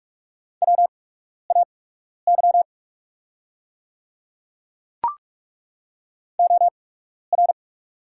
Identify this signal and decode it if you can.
Morse code, heard as a keyed tone, about 44 wpm, spaced as for 12 wpm, 690 Hz. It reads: WAY OR
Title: 8x Character Spacing - Top 200 Words - 45wpm